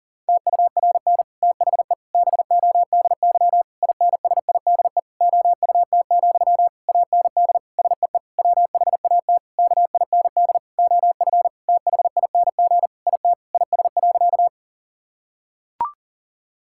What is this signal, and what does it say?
Morse code, heard as a keyed tone, about 40 wpm, 700 Hz. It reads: TURN THE BODY INSIDE OUT, AND SEE WHAT KIND OF THING IT IS.
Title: Sentences from Top 1000 Words - Review Part 6 - 40wpm